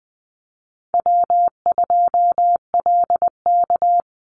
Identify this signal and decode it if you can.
Morse code, heard as a keyed tone, about 20 wpm, 700 Hz. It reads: W2LK
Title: Easy Call Signs - 20wpm